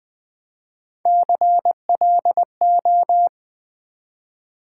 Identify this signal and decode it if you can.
Morse code, heard as a keyed tone, about 20 wpm, 700 Hz. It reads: CLO
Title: Build A Word - 1 Letter At A Time - 5-Letter Words - 20wpm